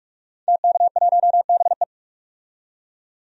Morse code, heard as a keyed tone, about 46 wpm, 700 Hz. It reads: TK1BE
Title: DXCC-100 Call Signs - 5 Characters - 45wpm